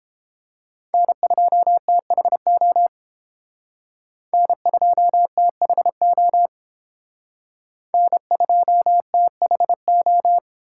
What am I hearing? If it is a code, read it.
Morse code, heard as a keyed tone, about 30 wpm, 700 Hz. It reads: N2THO N2THO N2THO